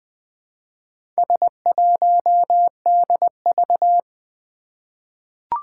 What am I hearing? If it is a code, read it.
Morse code, heard as a keyed tone, about 20 wpm, 700 Hz. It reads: S1DV
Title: Four Challenging Letter-Number Combinations 20wpm